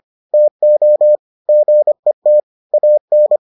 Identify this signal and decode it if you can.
Morse code, heard as a keyed tone, about 25 wpm, 600 Hz. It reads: TO GET AN